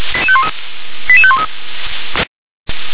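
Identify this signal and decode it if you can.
Five-tone selective call (ZVEI), heard as quick runs of five tones, tones of about 70 ms. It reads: 88513
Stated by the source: ZVEI five-tone call received on SDR.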